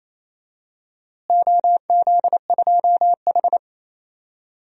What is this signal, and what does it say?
Morse code, heard as a keyed tone, about 28 wpm, 700 Hz. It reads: OZ2H
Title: DXCC-100 Call Signs - 4 Characters - Repeated 3x - 28wpm